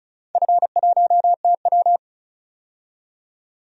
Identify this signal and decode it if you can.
Morse code, heard as a keyed tone, about 36 wpm, 700 Hz. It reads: F1TW
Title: DXCC-100 Call Signs - 4 Characters - 35wpm